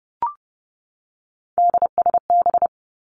Morse code, heard as a keyed tone, about 30 wpm, 700 Hz. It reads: DSB
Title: Three Challenging Letter-Number Combinations 30wpm